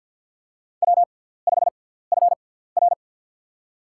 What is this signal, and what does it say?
Morse code, heard as a keyed tone, about 52 wpm, spaced as for 17 wpm, 690 Hz. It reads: W5FR
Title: USA Call Signs - 4 Characters - 6x Character Spacing - 50wpm